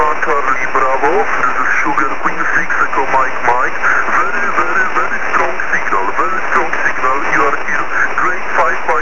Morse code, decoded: none

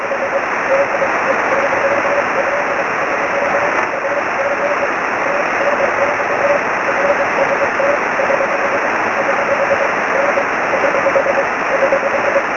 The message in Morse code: none

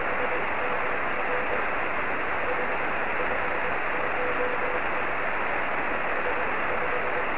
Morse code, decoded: none

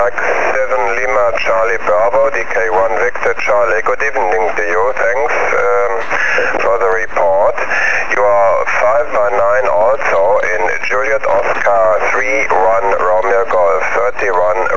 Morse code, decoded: none

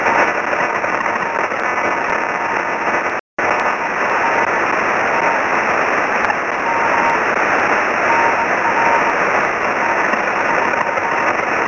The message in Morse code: none

